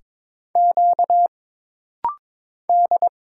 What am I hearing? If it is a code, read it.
Morse code, heard as a keyed tone, about 22 wpm, 700 Hz. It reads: Q D
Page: This rapid-fire timing is geared to push your ICR (Instant Character Recognition) skills.